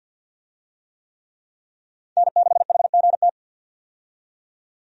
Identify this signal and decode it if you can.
Morse code, heard as a keyed tone, about 48 wpm, 700 Hz. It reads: N6HGT
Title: USA Call Signs - 5 Characters - 50wpm